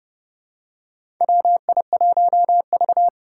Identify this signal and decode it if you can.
Morse code, heard as a keyed tone, about 30 wpm, 700 Hz. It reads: WI1V
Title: USA Call Signs - 2x Character Spacing - 30wpm